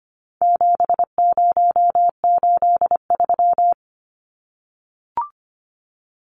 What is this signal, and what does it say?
Morse code, heard as a keyed tone, about 25 wpm, 700 Hz. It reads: 7083
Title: Numbers - 4-Digit 25wpm